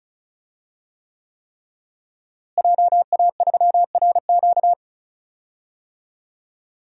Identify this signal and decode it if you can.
Morse code, heard as a keyed tone, about 34 wpm, 700 Hz. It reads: JA3RQ
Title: DXCC-100 Call Signs - 5 Characters - Repeated 3x - 35wpm